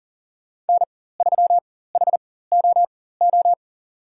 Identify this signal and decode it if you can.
Morse code, heard as a keyed tone, about 38 wpm, spaced as for 18 wpm, 700 Hz. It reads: N3HOO
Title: USA Call Signs - 5 Characters - 4x Character Spacing - 40wpm